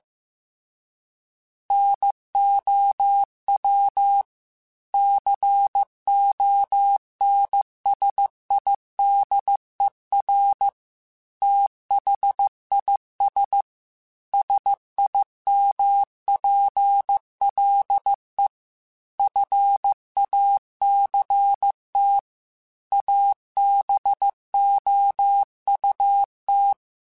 Morse code, heard as a keyed tone, about 15 wpm, 800 Hz. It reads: NOW CONSIDER THIS SIMPLE FACT ABOUT
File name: Technical Subjects / Technical Subjects-mp3s / OR Gate @13